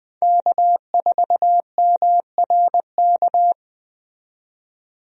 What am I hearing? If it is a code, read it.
Morse code, heard as a keyed tone, about 20 wpm, 700 Hz. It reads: K4MRK
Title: USA Call Signs - 5 Characters - 20wpm